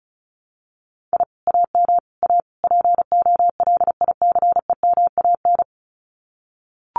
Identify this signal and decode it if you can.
Morse code, heard as a keyed tone, about 34 wpm, 700 Hz. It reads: I AM A POLICEMAN
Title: Common QSO Phrases Encore 35wpm